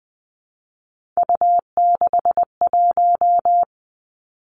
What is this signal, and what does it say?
Morse code, heard as a keyed tone, about 20 wpm, 700 Hz. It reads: U61
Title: Three Challenging Letter-Number Combinations 20wpm